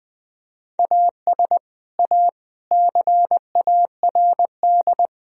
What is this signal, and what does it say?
Morse code, heard as a keyed tone, about 21 wpm, 700 Hz. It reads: AS A CARD
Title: Sentences from Top 1000 Words - Review Part 4 - 20wpm